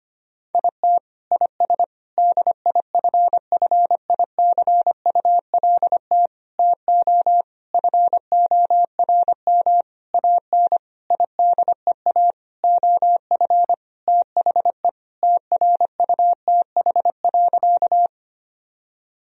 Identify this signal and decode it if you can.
Morse code, heard as a keyed tone, about 25 wpm, 700 Hz. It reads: IT IS DIFFICULT TO FORM AN IDEA OF THE TRUTH.
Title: Sentences from Top 1000 Words - Review Part 1 - 25wpm